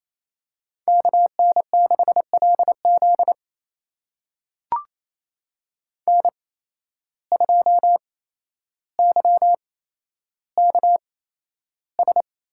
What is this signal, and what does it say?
Morse code, heard as a keyed tone, about 28 wpm, spaced as for 12 wpm, 700 Hz. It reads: KN6LZ N2YKS
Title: USA Call Signs - 5 Characters - 8x Character Spacing - 28wpm